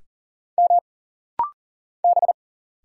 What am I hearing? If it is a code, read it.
Morse code, heard as a keyed tone, about 40 wpm, 700 Hz. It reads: M B
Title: Single Letter-Number Alt - Rapid-Fire 40wpm